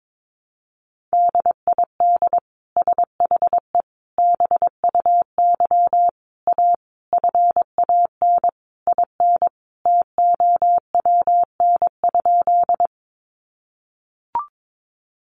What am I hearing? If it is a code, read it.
Morse code, heard as a keyed tone, about 22 wpm, 700 Hz. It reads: DID SHE BUY A FAN IN TOWN?
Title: Sentences from Top 1300 Words 22wpm